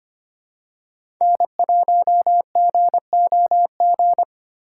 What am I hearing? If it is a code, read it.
Morse code, heard as a keyed tone, about 25 wpm, 700 Hz. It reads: N1GOG
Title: USA Call Signs - 5 Characters - 2x Character Spacing - 25wpm